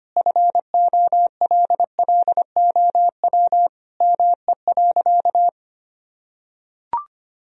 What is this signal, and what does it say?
Morse code, heard as a keyed tone, about 25 wpm, 700 Hz. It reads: FOLLOW ME.
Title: Sentences from Top 1000 Words - Review Part 1 - 25wpm